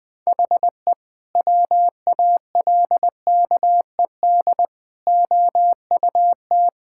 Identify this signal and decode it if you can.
Morse code, heard as a keyed tone, about 21 wpm, 700 Hz. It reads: HE WALKED OUT
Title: Sentences with 2-Word Phrases - 20wpm